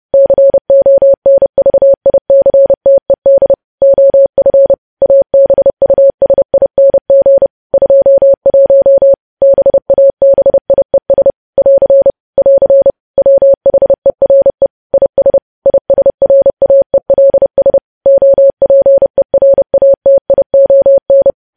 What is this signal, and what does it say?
Morse code, heard as a keyed tone, about 30 wpm, 550 Hz. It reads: CONVICTED OF ABUSING 21 BABIES <AR> <AR> WHERE IS ISRAELS OPERATION
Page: news headlines in morse code at 30 wpm